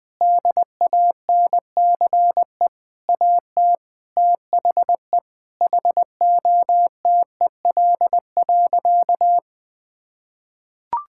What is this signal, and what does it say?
Morse code, heard as a keyed tone, about 21 wpm, 700 Hz. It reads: DANCE AT THE HOTEL.
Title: Sentences from Top 1600 Words 20wpm